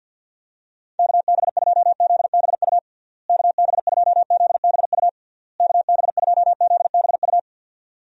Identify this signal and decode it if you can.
Morse code, heard as a keyed tone, about 52 wpm, 700 Hz. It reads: KB2ZBU KB2ZBU KB2ZBU
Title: USA Call Signs - 6 Characters - Repeated 3x - 50wpm